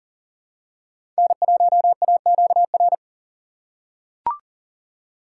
Morse code, heard as a keyed tone, about 40 wpm, 700 Hz. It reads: N1AQR
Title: USA Call Signs - 5 Characters - 2x Character Spacing - 40wpm